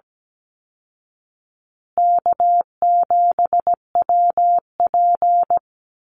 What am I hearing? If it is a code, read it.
Morse code, heard as a keyed tone, about 17 wpm, 700 Hz. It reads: K7WP
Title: Easy Call Signs Repeated 3x - 17wpm